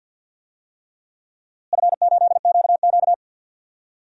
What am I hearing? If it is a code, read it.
Morse code, heard as a keyed tone, about 47 wpm, 700 Hz. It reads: F8QQ
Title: DXCC-100 Call Signs - 4 Characters - Repeated 3x - 50wpm